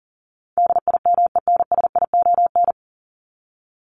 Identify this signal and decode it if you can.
Morse code, heard as a keyed tone, about 40 wpm, 700 Hz. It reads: DIMENSION
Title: Top 1501-1750 Words 40wpm